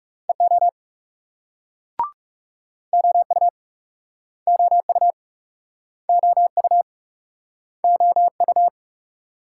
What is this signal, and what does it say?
Morse code, heard as a keyed tone, about 38 wpm, 700 Hz. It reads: EO OU OU OU OU